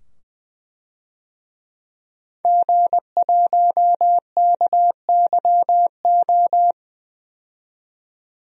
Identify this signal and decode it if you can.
Morse code, heard as a keyed tone, about 20 wpm, 700 Hz. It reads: G1KYO
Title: DXCC-100 Call Signs - 5 Characters - 20wpm